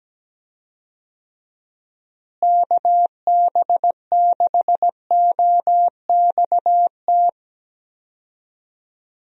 Morse code, heard as a keyed tone, about 17 wpm, 700 Hz. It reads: KB6OXT
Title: USA Call Signs - 6 Characters - Repeated 2x - 17wpm